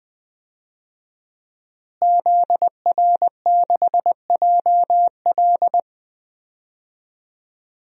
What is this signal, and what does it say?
Morse code, heard as a keyed tone, about 20 wpm, 700 Hz. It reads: ZR6JL
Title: DXCC-100 Call Signs - 5 Characters - Repeated 2x - 20wpm